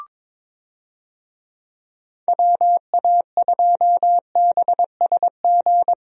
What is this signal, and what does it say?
Morse code, heard as a keyed tone, about 22 wpm, 700 Hz. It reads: WA2BSG
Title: USA Call Signs - 6 Characters - Repeated 2x - 22wpm